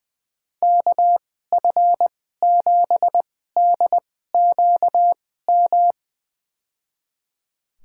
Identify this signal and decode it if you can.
Morse code, heard as a keyed tone, about 21 wpm, spaced as for 15 wpm, 700 Hz. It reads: KF7DQM
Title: USA Call Signs - 2x Character Spacing - 20wpm